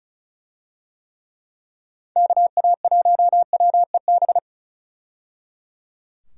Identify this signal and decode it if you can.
Morse code, heard as a keyed tone, about 36 wpm, 700 Hz. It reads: KA1WEB
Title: USA Call Signs - 6 Characters - 35wpm